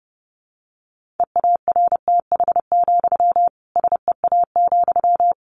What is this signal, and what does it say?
Morse code, heard as a keyed tone, about 30 wpm, 700 Hz. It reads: EARTH, SEA,
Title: Sentences from Top 1500 Words 30wpm